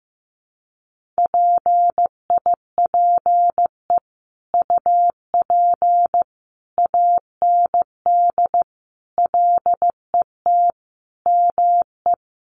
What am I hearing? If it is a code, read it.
Morse code, heard as a keyed tone, about 15 wpm, 700 Hz. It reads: PIPE UP AND LET ME
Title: Sentences from Top 1500 Words 15wpm